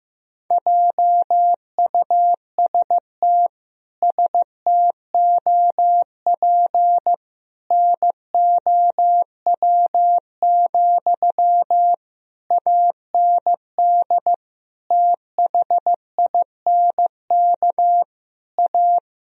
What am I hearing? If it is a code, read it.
Morse code, heard as a keyed tone, about 15 wpm, 700 Hz. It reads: JUST STOP NOW, AND THINK A
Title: Sentences from Top 1000 Words - Review Part 1 - 15wpm